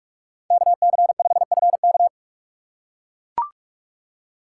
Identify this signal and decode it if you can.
Morse code, heard as a keyed tone, about 44 wpm, 700 Hz. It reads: KC5FK